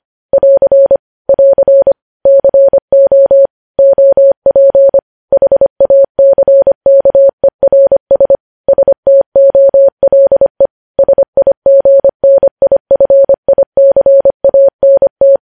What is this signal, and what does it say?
Morse code, heard as a keyed tone, about 25 wpm, 550 Hz. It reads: <AR> <AR> CO OP HACKERS STOLE SIGNIFICANT
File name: podcast / News headlines-1x-25x25